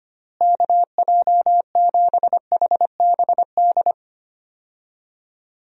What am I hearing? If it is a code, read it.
Morse code, heard as a keyed tone, about 25 wpm, 700 Hz. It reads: KJ7HBD
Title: USA Call Signs - 6 Characters - Repeated 2x - 25wpm